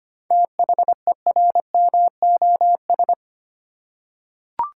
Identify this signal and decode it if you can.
Morse code, heard as a keyed tone, about 25 wpm, 690 Hz. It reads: THERMOS
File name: practice / 25wpm / Easy Words with Hints - 25wpm